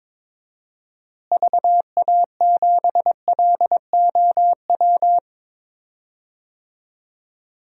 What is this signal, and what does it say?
Morse code, heard as a keyed tone, about 22 wpm, 700 Hz. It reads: VA7LOW
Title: DXCC-100 Call Signs - 6 Characters - Repeated 3x - 22wpm